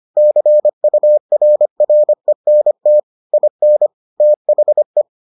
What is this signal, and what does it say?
Morse code, heard as a keyed tone, about 25 wpm, 600 Hz. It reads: CURRENT IN THE